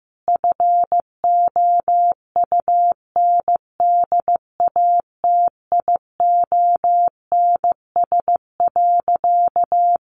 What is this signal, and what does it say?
Morse code, heard as a keyed tone, about 15 wpm, 700 Hz. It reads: FOUNDATIONS.
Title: A Sound Of Thunder 15wpm